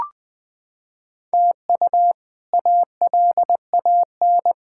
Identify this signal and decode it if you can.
Morse code, heard as a keyed tone, about 20 wpm, 700 Hz. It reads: TU ALAN
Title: K1USN SST QSO Practice - 20wpm